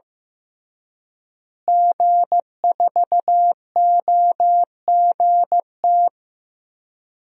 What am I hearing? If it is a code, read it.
Morse code, heard as a keyed tone, about 15 wpm, 700 Hz. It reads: G4OGT
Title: DXCC-100 Call Signs - 5 Characters - Repeated 2x - 15wpm